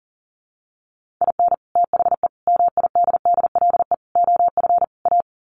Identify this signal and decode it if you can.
Morse code, heard as a keyed tone, about 40 wpm, 700 Hz. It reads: IN THE MIDDLE OF A
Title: Sets of 5 Words 40wpm